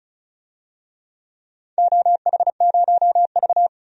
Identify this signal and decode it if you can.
Morse code, heard as a keyed tone, about 36 wpm, 700 Hz. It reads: OH0V
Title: DXCC-100 Call Signs - 4 Characters - 35wpm